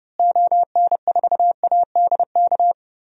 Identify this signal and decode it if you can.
Morse code, heard as a keyed tone, about 30 wpm, 700 Hz. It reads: ON4ADK
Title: DXCC-100 Call Signs - 6 Characters - Repeated 3x - 30wpm